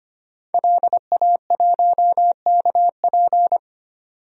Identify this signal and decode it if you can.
Morse code, heard as a keyed tone, about 25 wpm, 700 Hz. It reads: LA1KP
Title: DXCC-100 Call Signs - 5 Characters - 25wpm